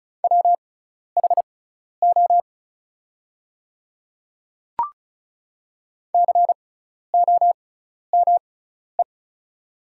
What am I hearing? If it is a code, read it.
Morse code, heard as a keyed tone, about 36 wpm, spaced as for 12 wpm, 700 Hz. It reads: WHO COME
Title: 6x Character Spacing - Top 200 Words - 35wpm